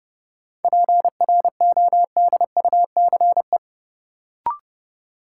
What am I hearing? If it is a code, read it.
Morse code, heard as a keyed tone, about 30 wpm, 700 Hz. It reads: PRODUCE
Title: Sentences from Top 1000 Words - Review Part 3 - New Word Drill - 30wpm